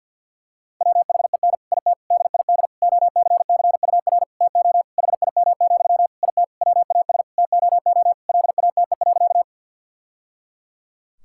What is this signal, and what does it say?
Morse code, heard as a keyed tone, about 50 wpm, 700 Hz. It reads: WHEN IT DID OCCUR TO HIM, IT WAS TOO LATE.
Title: Sentences from Top 1000 Words - Review Part 6 - 50wpm